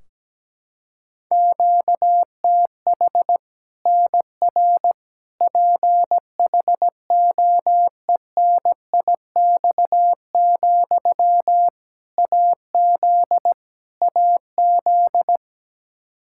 Morse code, heard as a keyed tone, about 17 wpm, 700 Hz. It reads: QTH NR PHOENIX, AZ AZ
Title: Practice QSO - Exchange 1 - Beginning - 17wpm